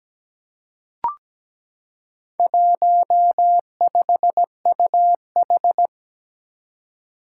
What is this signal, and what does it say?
Morse code, heard as a keyed tone, about 17 wpm, 700 Hz. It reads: 15UH
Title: Four Challenging Letter-Number Combinations 17wpm